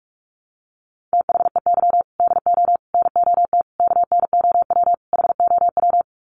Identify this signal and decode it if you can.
Morse code, heard as a keyed tone, about 44 wpm, 700 Hz. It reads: THEY DO NOT KNOW HOW
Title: Sets of 5 Words Encore2 45wpm